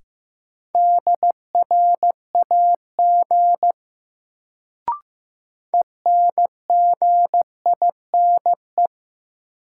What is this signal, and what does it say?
Morse code, heard as a keyed tone, about 15 wpm, 700 Hz. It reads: DRAG ENGINE
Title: Sentences from Top 1500 Words - New Word Drill Alt - 15wpm